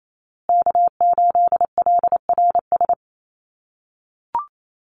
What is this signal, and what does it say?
Morse code, heard as a keyed tone, about 28 wpm, 700 Hz. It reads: K8LRS
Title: USA Call Signs Repeated 2x 28wpm